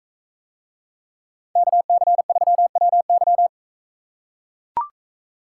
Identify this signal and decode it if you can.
Morse code, heard as a keyed tone, about 44 wpm, 700 Hz. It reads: KC3WY